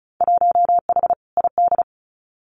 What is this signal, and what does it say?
Morse code, heard as a keyed tone, about 36 wpm, 700 Hz. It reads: 1H ID
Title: Winter Field Day Exchanges 35wpm